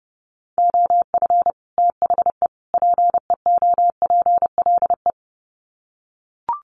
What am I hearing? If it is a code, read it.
Morse code, heard as a keyed tone, about 30 wpm, 700 Hz. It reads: OF THE PEOPLE
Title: Sets of 3 Words 30wpm